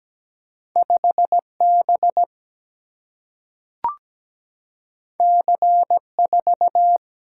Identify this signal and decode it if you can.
Morse code, heard as a keyed tone, about 17 wpm, 700 Hz. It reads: 5 B C 4
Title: Single-Letter Single-Number 17wpm